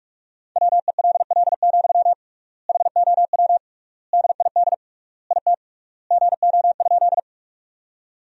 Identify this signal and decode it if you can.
Morse code, heard as a keyed tone, about 46 wpm, 700 Hz. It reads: WELL, HOW DID IT GO?
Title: 3x Word-Spacing - Sentences from Top 200 Words - 45wpm